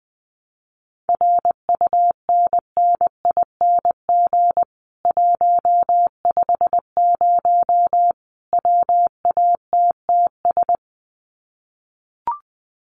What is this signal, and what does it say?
Morse code, heard as a keyed tone, about 20 wpm, 700 Hz. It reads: RUNNING 150 WATTS
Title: Common QSO Phrases 20wpm